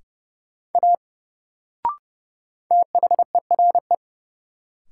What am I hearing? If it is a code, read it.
Morse code, heard as a keyed tone, about 30 wpm, 700 Hz. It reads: A THERE
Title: Sentences from Top 100 Words - New Word Drill - 30wpm